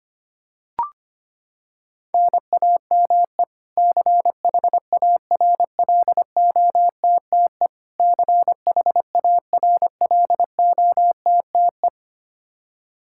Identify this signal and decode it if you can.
Morse code, heard as a keyed tone, about 25 wpm, 700 Hz. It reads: NAME CHARLOTTE CHARLOTTE
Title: Practice QSO - Exchange 1 - Beginning - 25wpm